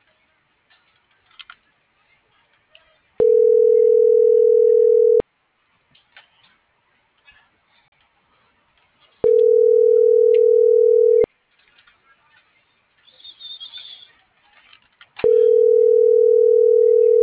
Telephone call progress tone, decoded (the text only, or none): ringback tone